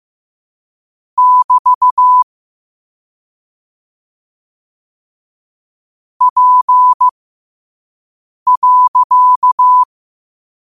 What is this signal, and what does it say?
Morse code, heard as a keyed tone, about 15 wpm, spaced as for 5 wpm, 1000 Hz. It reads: <BT> P.
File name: 5wpm